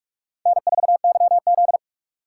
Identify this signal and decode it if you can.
Morse code, heard as a keyed tone, about 38 wpm, 700 Hz. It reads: N4YZ